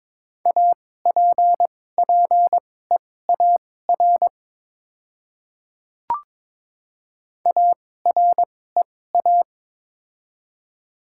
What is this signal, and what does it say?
Morse code, heard as a keyed tone, about 22 wpm, spaced as for 16 wpm, 700 Hz. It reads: APPEAR AREA
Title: 2x Character Spacing - Top 500 Words - 22wpm